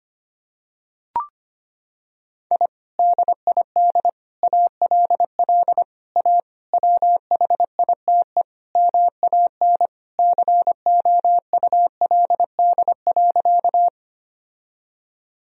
Morse code, heard as a keyed tone, about 25 wpm, 700 Hz. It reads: I DID ALL A WHITE MAN COULD.
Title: Sentences from Top 1000 Words - Review Part 1 - 25wpm